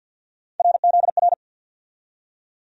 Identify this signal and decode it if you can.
Morse code, heard as a keyed tone, about 53 wpm, 700 Hz. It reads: RZR
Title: USA Call Sign Postfixes - 3 Letters 50wpm